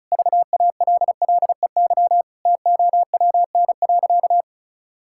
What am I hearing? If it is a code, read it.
Morse code, heard as a keyed tone, about 34 wpm, 700 Hz. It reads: VALLEY TOWN.
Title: Pangram Challenge v2 - 35wpm